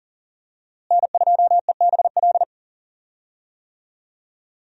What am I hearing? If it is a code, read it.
Morse code, heard as a keyed tone, about 39 wpm, 700 Hz. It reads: N2EBL
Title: USA Call Signs - 5 Characters - Repeated 3x - 40wpm